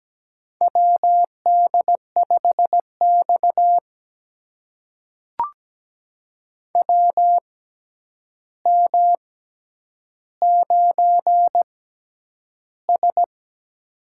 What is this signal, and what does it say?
Morse code, heard as a keyed tone, about 17 wpm, spaced as for 9 wpm, 700 Hz. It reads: WD5X WM9S
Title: USA Call Signs - 4 Characters - 6x Character Spacing - 17wpm